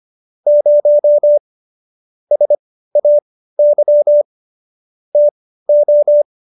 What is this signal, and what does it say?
Morse code, heard as a keyed tone, about 25 wpm, spaced as for 15 wpm, 590 Hz. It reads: 0 SAY TO
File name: Jokes-25-char-15-eff